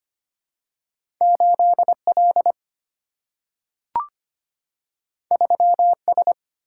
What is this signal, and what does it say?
Morse code, heard as a keyed tone, about 25 wpm, 700 Hz. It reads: 8L 3S
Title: Single-Letter Single-Number 25wpm